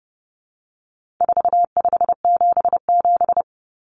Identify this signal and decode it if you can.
Morse code, heard as a keyed tone, about 30 wpm, 700 Hz. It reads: 4577
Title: Common Cut Numbers - 4-Digit 30wpm